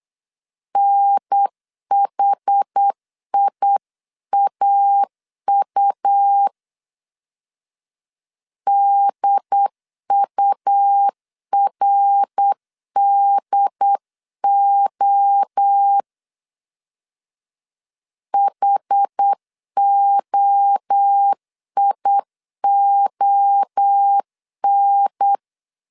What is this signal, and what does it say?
Morse code, heard as a keyed tone, about 8 wpm, 800 Hz. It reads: NHIAU DURDO HOION